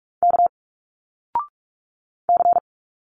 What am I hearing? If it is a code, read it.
Morse code, heard as a keyed tone, about 46 wpm, 700 Hz. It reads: K C
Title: Single Letters 45wpm